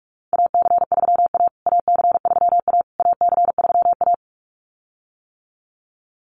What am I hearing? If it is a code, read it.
Morse code, heard as a keyed tone, about 44 wpm, 700 Hz. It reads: AC3A AC3A AC3A
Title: USA Call Signs Repeated 3x 45wpm